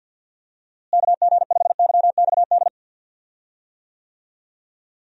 Morse code, heard as a keyed tone, about 49 wpm, 700 Hz. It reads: KG5YXD
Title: USA Call Signs - 6 Characters - Repeated 3x - 50wpm